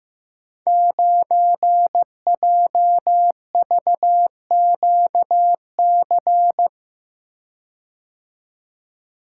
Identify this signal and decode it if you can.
Morse code, heard as a keyed tone, about 15 wpm, 700 Hz. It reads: 9JVQC
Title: Code Groups - 5 Letter:Numbers - 15wpm